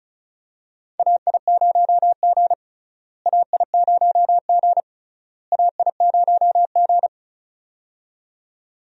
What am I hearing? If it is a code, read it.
Morse code, heard as a keyed tone, about 36 wpm, 700 Hz. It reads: AI0G AI0G AI0G
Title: USA Call Signs - 4 Characters - Repeated 3x - 35wpm